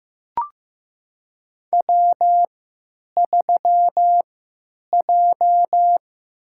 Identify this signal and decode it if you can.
Morse code, heard as a keyed tone, about 15 wpm, spaced as for 9 wpm, 700 Hz. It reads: W3J
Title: USA Call Signs - 3x Character Spacing - 15wpm